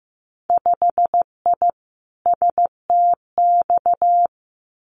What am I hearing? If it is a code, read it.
Morse code, heard as a keyed tone, about 15 wpm, 700 Hz. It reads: 5I STX
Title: Winter Field Day Exchanges Encore 15wpm